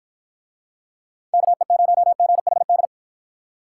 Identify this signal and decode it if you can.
Morse code, heard as a keyed tone, about 49 wpm, 700 Hz. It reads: KE0GHD